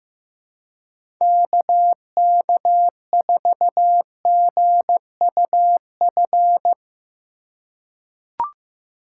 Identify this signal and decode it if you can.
Morse code, heard as a keyed tone, about 15 wpm, 700 Hz. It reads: KK4GUF
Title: USA Call Signs - 6 Characters - 4x Character Spacing - 15wpm